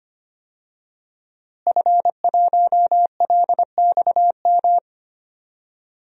Morse code, heard as a keyed tone, about 25 wpm, 700 Hz. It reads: F1LXM
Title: DXCC-100 Call Signs - 5 Characters - Repeated 2x - 25wpm